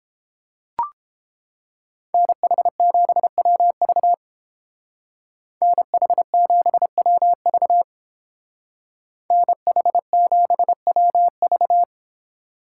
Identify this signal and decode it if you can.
Morse code, heard as a keyed tone, about 30 wpm, 700 Hz. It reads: NH7WV NH7WV NH7WV